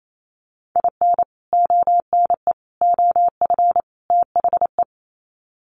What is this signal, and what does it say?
Morse code, heard as a keyed tone, about 28 wpm, 700 Hz. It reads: IN ONE OF THE
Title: Sets of 4 Words 28wpm